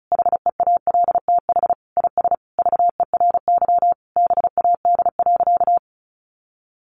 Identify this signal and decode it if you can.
Morse code, heard as a keyed tone, about 34 wpm, 700 Hz. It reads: HEALTH IS VERY BAD.
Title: Sentences from Top 1200 Words 35wpm